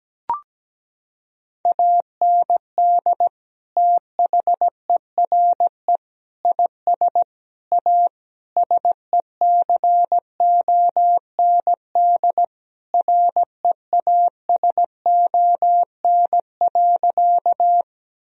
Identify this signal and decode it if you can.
Morse code, heard as a keyed tone, about 17 wpm, 700 Hz. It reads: AND THERE IS A SECOND REASON.
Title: Sentences from Top 1000 Words - Review Part 1 - 17wpm